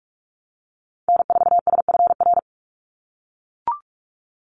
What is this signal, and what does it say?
Morse code, heard as a keyed tone, about 44 wpm, 700 Hz. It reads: N4SFR
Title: NA SOTA Activator Callsigns 45wpm